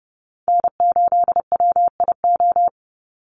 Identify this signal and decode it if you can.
Morse code, heard as a keyed tone, about 32 wpm, 700 Hz. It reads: N8WIO
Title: USA Call Signs 30wpm